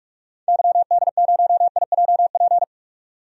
Morse code, heard as a keyed tone, about 46 wpm, 700 Hz. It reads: YD0IJP
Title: DXCC-100 Call Signs - 6 Characters - 45wpm